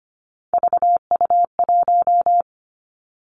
Morse code, heard as a keyed tone, about 25 wpm, 700 Hz. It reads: VU1
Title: Three Challenging Letter-Number Combinations 25wpm